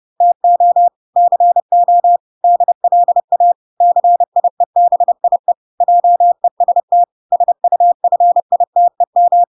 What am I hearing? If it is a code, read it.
Morse code, heard as a keyed tone, about 30 wpm, 700 Hz. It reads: TO CO DLA CIEBIE JEST SUFITEM